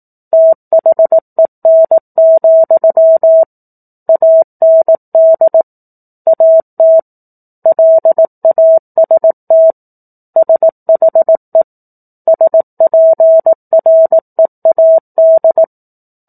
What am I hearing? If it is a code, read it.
Morse code, heard as a keyed tone, about 18 wpm, 650 Hz. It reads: THEN, AND AT LAST SHE SPREAD
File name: 04-Chapter-04 18 WPM